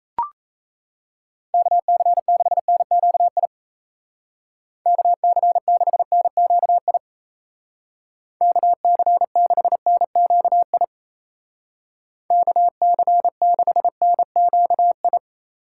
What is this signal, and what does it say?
Morse code, heard as a keyed tone, about 37 wpm, 700 Hz. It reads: KC6NQI KC6NQI KC6NQI KC6NQI